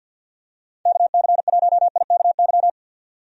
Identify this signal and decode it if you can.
Morse code, heard as a keyed tone, about 47 wpm, 700 Hz. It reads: KC1IKY
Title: USA Call Signs - 6 Characters - 50wpm